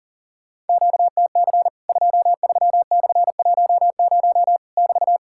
Morse code, heard as a keyed tone, about 40 wpm, 700 Hz. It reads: QTC 23/10 <BT>
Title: Worked All Europe DX Contest - QTC Exchange - 40wpm